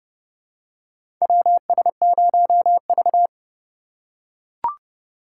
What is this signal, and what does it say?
Morse code, heard as a keyed tone, about 30 wpm, 700 Hz. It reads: WS0V
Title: USA Call Signs - 3x Character Spacing - 30wpm